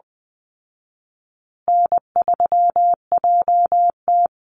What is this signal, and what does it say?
Morse code, heard as a keyed tone, about 20 wpm, 700 Hz. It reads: N3JT
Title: Easy Call Signs Repeated 3x - 20wpm